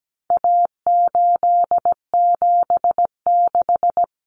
Morse code, heard as a keyed tone, about 17 wpm, 700 Hz. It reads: A876
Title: Common Cut Numbers - 4-Digit 17wpm